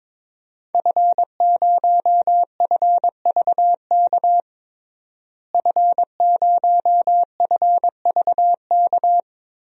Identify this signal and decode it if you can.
Morse code, heard as a keyed tone, about 22 wpm, 700 Hz. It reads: F0FVK F0FVK
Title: DXCC-100 Call Signs - 5 Characters - Repeated 3x - 22wpm